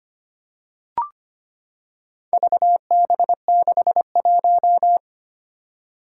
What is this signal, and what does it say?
Morse code, heard as a keyed tone, about 25 wpm, 700 Hz. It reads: VB61
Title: Four Challenging Letter-Number Combinations 25wpm